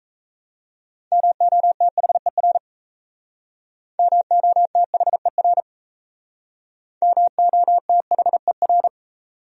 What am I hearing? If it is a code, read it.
Morse code, heard as a keyed tone, about 35 wpm, 700 Hz. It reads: MOTHER MOTHER MOTHER
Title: Speed-Racing Top 200 Words 28wpm